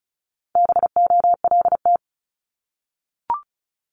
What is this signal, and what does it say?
Morse code, heard as a keyed tone, about 36 wpm, 700 Hz. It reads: BOLT
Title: Multi-letter Phonemes Repeated 3x 35wpm